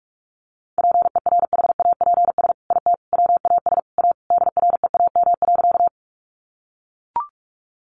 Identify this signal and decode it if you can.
Morse code, heard as a keyed tone, about 44 wpm, 700 Hz. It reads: PERHAPS IT WAS A DREAM.
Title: Sentences from Top 1500 Words 45wpm